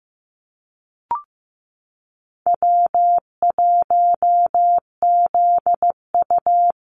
Morse code, heard as a keyed tone, about 15 wpm, 700 Hz. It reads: W1ZU
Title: NA SOTA Activator Callsigns 15wpm